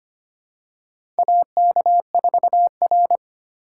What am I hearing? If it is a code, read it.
Morse code, heard as a keyed tone, about 25 wpm, 700 Hz. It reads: AK4R
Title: USA Call Signs - 4 Characters - 2x Character Spacing - 25wpm